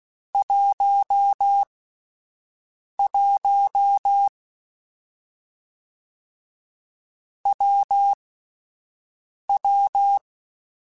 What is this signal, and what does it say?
Morse code, heard as a keyed tone, about 16 wpm, spaced as for 5 wpm, 800 Hz. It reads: 11 WW
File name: Difficult Combos / Difficult Combos-mp3s / WPJ WPJ1 @5